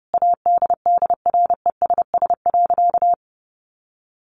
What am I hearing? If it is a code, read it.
Morse code, heard as a keyed tone, about 30 wpm, 700 Hz. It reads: ADDRESS.
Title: Sentences from Top 1100 Words 30wpm